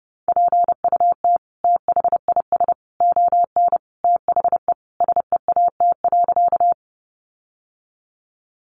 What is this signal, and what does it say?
Morse code, heard as a keyed tone, about 30 wpm, 700 Hz. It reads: PUT THIS ON THE SEAT.
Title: Sentences from Top 1500 Words 30wpm